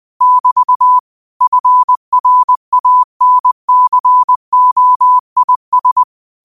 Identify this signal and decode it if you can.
Morse code, heard as a keyed tone, about 20 wpm, 1000 Hz. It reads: <BT> FRANCOIS
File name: qotdfiles / 20wpm